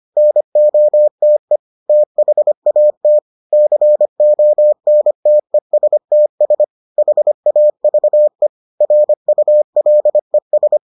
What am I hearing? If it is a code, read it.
Morse code, heard as a keyed tone, about 25 wpm, 600 Hz. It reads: NOTE THAT CONTESTS HAVE RULES